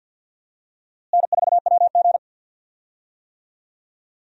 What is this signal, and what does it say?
Morse code, heard as a keyed tone, about 49 wpm, 700 Hz. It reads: N4WG
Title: USA Call Signs - 4 Characters - Repeated 2x - 50wpm